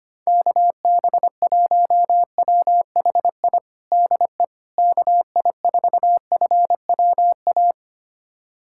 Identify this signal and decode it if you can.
Morse code, heard as a keyed tone, about 25 wpm, 700 Hz. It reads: KB1WHI DE KI4FWA
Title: USA Callsign DE Callsign - 6 Characters - 25wpm